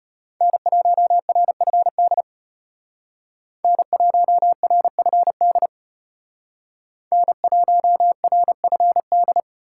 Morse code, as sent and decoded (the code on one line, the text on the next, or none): -. .---- .-. ..-. -.. / -. .---- .-. ..-. -.. / -. .---- .-. ..-. -..
N1RFD N1RFD N1RFD